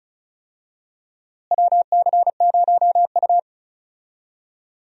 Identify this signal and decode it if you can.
Morse code, heard as a keyed tone, about 36 wpm, 700 Hz. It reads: WC0U
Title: USA Call Signs - 4 Characters - Repeated 2x - 35wpm